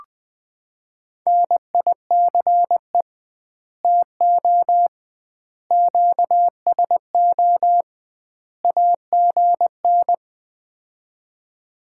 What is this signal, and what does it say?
Morse code, heard as a keyed tone, about 20 wpm, 700 Hz. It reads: NICE TO QSO AGN
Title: Common QSO Phrases - 2x Word-Spacing - 20wpm